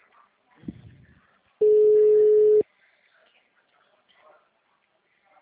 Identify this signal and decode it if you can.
Ringback tone (Europe, 425 Hz), with speech.